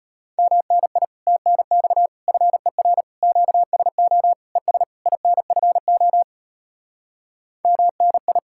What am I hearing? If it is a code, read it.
Morse code, heard as a keyed tone, about 33 wpm, 700 Hz. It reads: MNI TNX FER QSO ES INFO MNI